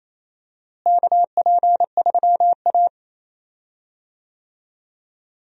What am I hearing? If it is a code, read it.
Morse code, heard as a keyed tone, about 28 wpm, 700 Hz. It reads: KP3A
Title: USA Call Signs - 4 Characters - Repeated 2x - 28wpm